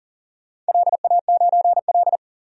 Morse code, heard as a keyed tone, about 42 wpm, 700 Hz. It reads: LA9L
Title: DXCC-100 Call Signs - 4 Characters - 40wpm